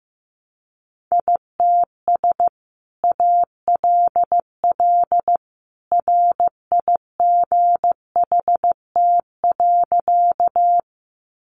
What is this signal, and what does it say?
Morse code, heard as a keyed tone, about 15 wpm, 700 Hz. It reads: ITS ALL RIGHT.
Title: Sentences from Top 600 Words 15wpm